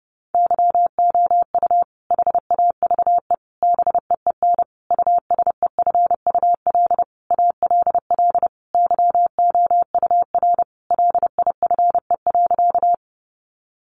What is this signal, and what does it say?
Morse code, heard as a keyed tone, about 30 wpm, 700 Hz. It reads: YOU HAVE BEEN USEFUL ALL YOUR LIFE.
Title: Sentences from Top 700 Words 30wpm